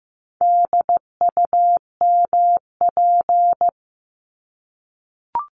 Morse code, heard as a keyed tone, about 15 wpm, 700 Hz. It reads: DUMP
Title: Top 1001-1250 Words 15wpm